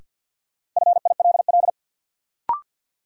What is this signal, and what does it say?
Morse code, heard as a keyed tone, about 53 wpm, 700 Hz. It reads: FILL
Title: Sentences from Top 1000 Words - Review Part 3 - New Word Drill - 50wpm